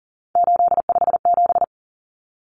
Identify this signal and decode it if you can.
Morse code, heard as a keyed tone, about 40 wpm, 700 Hz. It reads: 857
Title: Common Cut Numbers - 3-Digit 40wpm